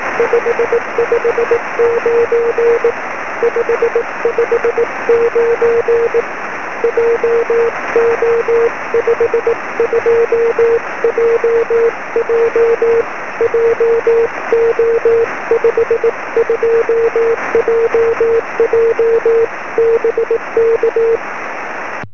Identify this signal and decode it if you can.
Morse code, heard as a keyed tone, about 18 wpm, 470 Hz, with noise. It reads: 559 559 JO52JJ JO52JJ BK
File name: sm7lcb / 07-dk6as 23cm